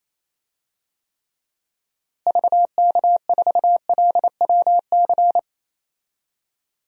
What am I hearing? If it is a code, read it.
Morse code, heard as a keyed tone, about 28 wpm, 700 Hz. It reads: VK4LWC
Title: DXCC-100 Call Signs - 6 Characters - Repeated 3x - 28wpm